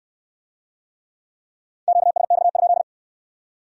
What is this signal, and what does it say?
Morse code, heard as a keyed tone, about 62 wpm, 700 Hz. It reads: QSX?